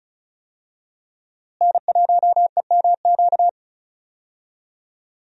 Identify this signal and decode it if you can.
Morse code, heard as a keyed tone, about 36 wpm, 700 Hz. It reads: N1EMQ
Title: USA Call Signs - 5 Characters - Repeated 2x - 35wpm